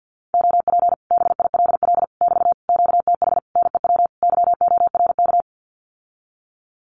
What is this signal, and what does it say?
Morse code, heard as a keyed tone, about 50 wpm, 700 Hz. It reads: OP BILL <BT> QTH NEW YORK
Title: Common QSO Phrases Encore 50wpm